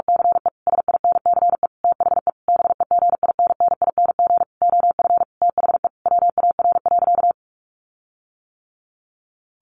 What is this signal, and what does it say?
Morse code, heard as a keyed tone, about 44 wpm, 700 Hz. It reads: CE SINCE THE BEGINNING OF THE WAR.
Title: Sentences from Top 1400 Words 45wpm